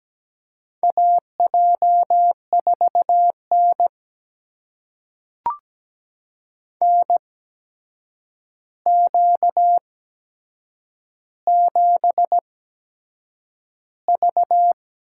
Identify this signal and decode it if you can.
Morse code, heard as a keyed tone, about 17 wpm, spaced as for 7 wpm, 700 Hz. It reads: AJ4N NQ7V
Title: USA Call Signs - 4 Characters - 8x Character Spacing - 17wpm